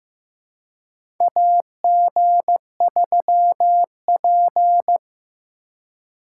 Morse code, heard as a keyed tone, about 15 wpm, 700 Hz. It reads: AG3P
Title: USA Call Signs - 4 Characters - 2x Character Spacing - 15wpm